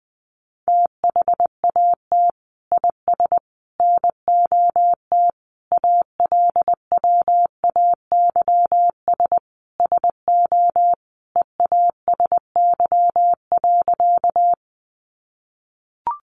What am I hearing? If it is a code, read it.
Morse code, heard as a keyed tone, about 20 wpm, 700 Hz. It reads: THAT IS NOT ALWAYS SO EASY.
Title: Sentences from Top 700 Words 20wpm